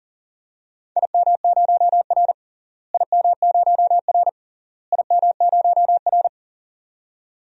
Morse code, heard as a keyed tone, about 40 wpm, 700 Hz. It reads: IM0R IM0R IM0R
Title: DXCC-100 Call Signs - 4 Characters - Repeated 3x - 40wpm